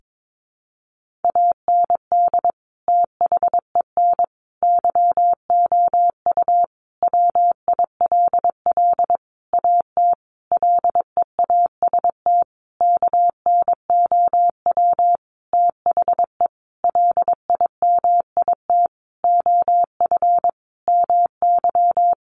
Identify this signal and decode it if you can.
Morse code, heard as a keyed tone, about 22 wpm, 700 Hz. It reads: AND THEN YOU WILL AT LEAST KNOW THE LIMIT OF MY
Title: Sentences from Top 1500 Words 22wpm